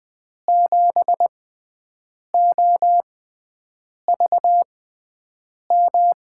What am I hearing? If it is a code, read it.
Morse code, heard as a keyed tone, about 20 wpm, spaced as for 7 wpm, 700 Hz. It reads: 7OVM
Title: USA Call Signs - 6 Characters - 6x Character Spacing - 20wpm